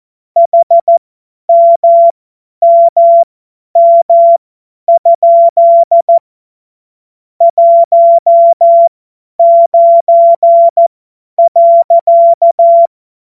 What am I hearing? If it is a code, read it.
Morse code, heard as a keyed tone, about 14 wpm, spaced as for 10 wpm, 680 Hz. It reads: HMMM? 19.
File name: Morsekurs / Witze / Witz 14. März 2023 50 F70